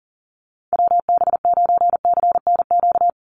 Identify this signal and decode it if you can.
Morse code, heard as a keyed tone, about 41 wpm, 700 Hz. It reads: WB9CNQ
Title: USA Call Signs 40wpm